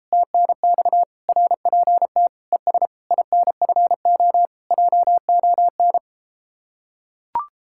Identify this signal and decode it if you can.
Morse code, heard as a keyed tone, about 33 wpm, 700 Hz. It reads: TNX RPT ES INFO JON